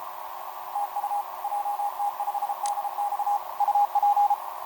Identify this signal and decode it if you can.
Morse code, heard as a keyed tone, about 33 wpm, 820 Hz. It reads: TU JT5DX UP
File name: jt5dx f5idj etc 12m cw 18mar2011 0915